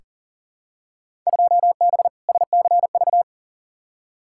Morse code, heard as a keyed tone, about 37 wpm, 700 Hz. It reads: 2B SCV
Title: Field Day Exchanges Encore - 40wpm